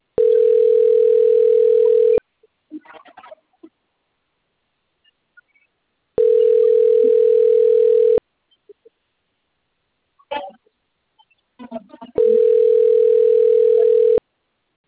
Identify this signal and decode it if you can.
Ringback tone (North America), 440 + 480 Hz.
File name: out- - - - - .25761